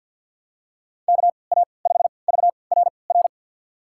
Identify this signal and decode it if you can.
Morse code, heard as a keyed tone, about 51 wpm, spaced as for 28 wpm, 700 Hz. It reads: KA5VRR
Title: USA Call Signs - 6 Characters - 3x Character Spacing - 50wpm